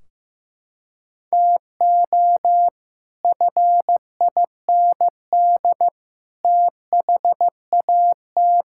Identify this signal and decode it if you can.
Morse code, heard as a keyed tone, about 15 wpm, 700 Hz. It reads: TO FIND THAT
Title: Sentences from Top 1600 Words 15wpm